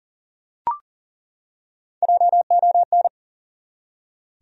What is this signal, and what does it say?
Morse code, heard as a keyed tone, about 40 wpm, 700 Hz. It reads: JON
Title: USA Common Names - Max 4 Letters - 40wpm